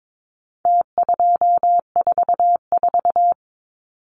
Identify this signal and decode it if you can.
Morse code, heard as a keyed tone, about 22 wpm, 700 Hz. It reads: T244
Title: Numbers - Contest Style - 4-Digit 22wpm